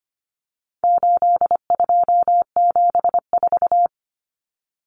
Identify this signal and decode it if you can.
Morse code, heard as a keyed tone, about 25 wpm, 700 Hz. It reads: 8274
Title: Numbers - 4-Digit 25wpm